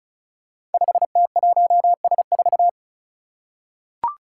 Morse code, heard as a keyed tone, about 36 wpm, 700 Hz. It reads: 5T1S4
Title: Five Challenging Letter-Number Combinations 35wpm